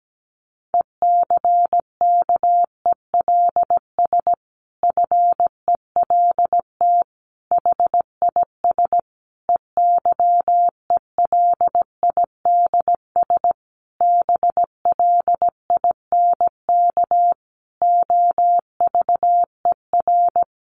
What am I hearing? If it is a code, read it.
Morse code, heard as a keyed tone, about 17 wpm, 700 Hz. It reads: ECKELS FELT HIS EYELIDS BLINK OVER
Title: A Sound Of Thunder 17wpm